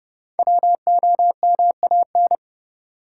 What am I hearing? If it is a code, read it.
Morse code, heard as a keyed tone, about 31 wpm, 700 Hz. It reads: WOMAN
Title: Sentences from Top 1100 Words - New Word Drill Alt - 30wpm